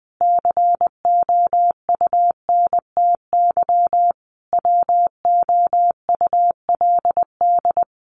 Morse code, heard as a keyed tone, about 20 wpm, 700 Hz. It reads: COUNTY WOULD
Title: Sentences from Top 1200 Words 20wpm